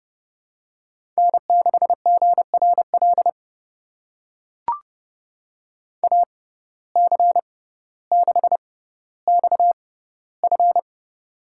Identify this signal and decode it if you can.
Morse code, heard as a keyed tone, about 29 wpm, spaced as for 15 wpm, 700 Hz. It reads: N6GRL AC6XF
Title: USA Call Signs - 5 Characters - 6x Character Spacing - 30wpm